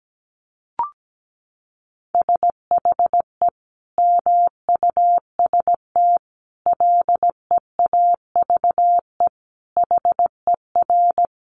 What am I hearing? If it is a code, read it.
Morse code, heard as a keyed tone, about 17 wpm, 700 Hz. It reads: SHE MUST LEAVE HER
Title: Sentences from Top 1300 Words 17wpm